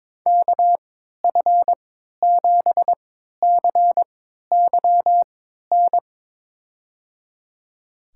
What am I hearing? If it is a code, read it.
Morse code, heard as a keyed tone, about 22 wpm, spaced as for 13 wpm, 700 Hz. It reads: KF7CYN
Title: USA Call Signs - 6 Characters - 3x Character Spacing - 22wpm